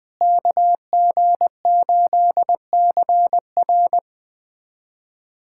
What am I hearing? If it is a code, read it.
Morse code, heard as a keyed tone, about 20 wpm, 700 Hz. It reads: KG8CR
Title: USA Callsign DE Callsign - 5 Characters - 20wpm